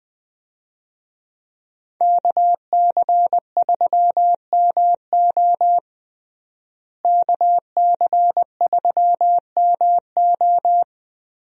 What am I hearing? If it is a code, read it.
Morse code, heard as a keyed tone, about 20 wpm, 700 Hz. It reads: KC3MO KC3MO
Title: USA Call Signs - 5 Characters - Repeated 3x - 20wpm